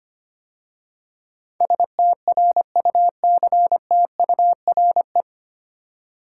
Morse code, heard as a keyed tone, about 25 wpm, 700 Hz. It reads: STRUCTURE
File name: practice / 25wpm / 9 Letter Words - First 3 Letters Given - 25wpm